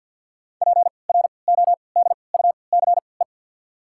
Morse code, heard as a keyed tone, about 47 wpm, spaced as for 28 wpm, 700 Hz. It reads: PRODUCE
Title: 3x Character Spacing - Top 500 Words - 50wpm